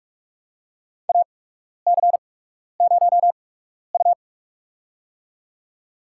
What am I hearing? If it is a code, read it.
Morse code, heard as a keyed tone, about 48 wpm, spaced as for 12 wpm, 700 Hz. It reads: AC0U
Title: USA Call Signs - 4 Characters - 8x Character Spacing - 45wpm